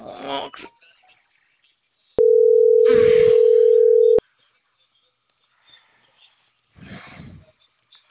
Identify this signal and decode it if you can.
Ringback tone (North America), 440 + 480 Hz.